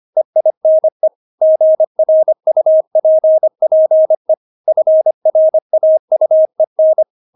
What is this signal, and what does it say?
Morse code, heard as a keyed tone, about 25 wpm, 630 Hz. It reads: EINE GRUPPE FRAUEN
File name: Witz6 25wpm